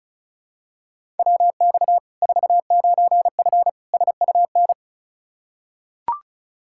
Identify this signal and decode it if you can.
Morse code, heard as a keyed tone, about 36 wpm, 700 Hz. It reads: WX 49F SUN
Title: Practice QSO - Exchange 2 - Info - 35wpm